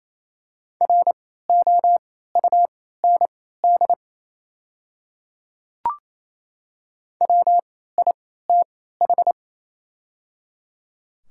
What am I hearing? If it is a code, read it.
Morse code, heard as a keyed tone, about 28 wpm, spaced as for 16 wpm, 700 Hz. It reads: ROUND WITH
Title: 3x Character Spacing - Top 200 Words - 28wpm